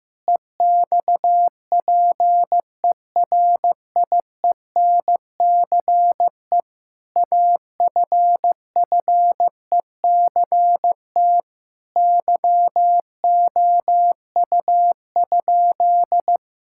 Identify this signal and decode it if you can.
Morse code, heard as a keyed tone, about 15 wpm, 700 Hz. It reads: EXPERIENCE AFFECT YOU?
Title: Sentences from Top 1000 Words - Review Part 6 - 15wpm